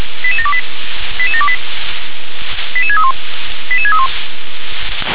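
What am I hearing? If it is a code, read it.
Five-tone selective call (ZVEI), heard as quick runs of five tones, tones of about 70 ms. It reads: 88728 88728 88631 88631
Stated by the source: ZVEI five-tone selective call 88000, received by SDR.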